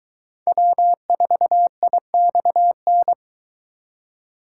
Morse code, heard as a keyed tone, about 21 wpm, 700 Hz. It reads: W4IXN